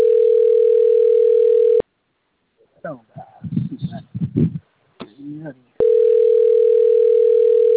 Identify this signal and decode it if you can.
Ringback tone (North America), 440 + 480 Hz, with speech.
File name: rg-1600-unknown- - - .77159